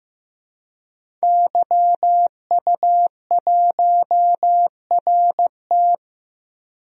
Morse code, heard as a keyed tone, about 15 wpm, 700 Hz. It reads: YU1RT
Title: DXCC-100 Call Signs - 5 Characters - Repeated 3x - 15wpm